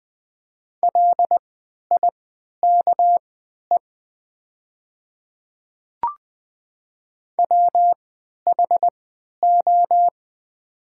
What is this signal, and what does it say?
Morse code, heard as a keyed tone, about 20 wpm, spaced as for 11 wpm, 700 Hz. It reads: LIKE WHO
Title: 3x Character Spacing - Top 100 Words - 20wpm